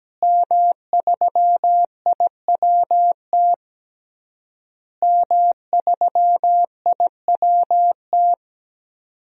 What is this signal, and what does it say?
Morse code, heard as a keyed tone, about 17 wpm, 700 Hz. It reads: M3IWT M3IWT
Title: DXCC-100 Call Signs - 5 Characters - Repeated 3x - 17wpm